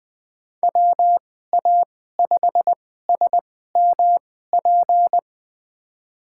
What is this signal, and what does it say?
Morse code, heard as a keyed tone, about 20 wpm, spaced as for 14 wpm, 700 Hz. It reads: WA5SMP
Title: USA Call Signs - 2x Character Spacing - 20wpm